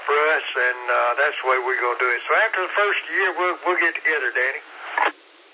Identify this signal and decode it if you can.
Emergency Alert System (SAME) tones, heard as harsh buzzing bursts, with speech. It is heard from 0.3 to 4.4 s.